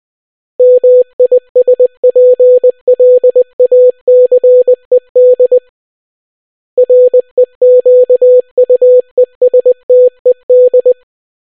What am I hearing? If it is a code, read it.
Morse code, heard as a keyed tone, about 20 wpm, 500 Hz. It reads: MISPLACED REQUESTED